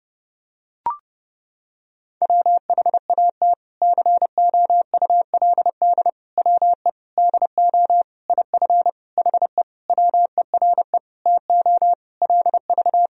Morse code, heard as a keyed tone, about 32 wpm, 700 Hz. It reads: WHAT COULD WE DO IF HE WERE TO LV
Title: Sentences from Top 400 Words and Abbreviations 30wpm